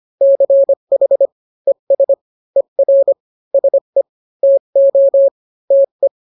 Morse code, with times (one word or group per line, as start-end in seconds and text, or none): 0.2-1.3 CH
1.7-2.1 ES
2.6-3.1 ER
3.5-4.0 SE
4.4-5.3 TO
5.7-6.1 TE